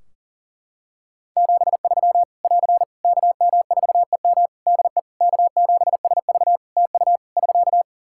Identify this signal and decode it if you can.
Morse code, heard as a keyed tone, about 40 wpm, 700 Hz. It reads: 73 <AR> KM4EM DE K7SV TU <SK>
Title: Practice QSO - Exchange 4 - Ending - 40wpm